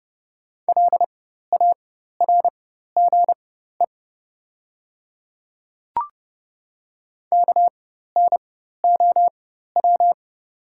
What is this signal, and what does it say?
Morse code, heard as a keyed tone, about 30 wpm, spaced as for 14 wpm, 700 Hz. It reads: LARGE KNOW
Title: 4x Character Spacing - Top 300 Words - 30wpm